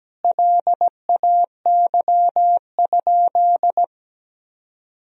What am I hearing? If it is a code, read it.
Morse code, heard as a keyed tone, about 17 wpm, 700 Hz. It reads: LAY?
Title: Sentences from Top 400 Words and Abbreviations 17wpm